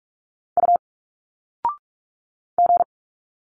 Morse code, heard as a keyed tone, about 46 wpm, 700 Hz. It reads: U G
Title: Single Letters 45wpm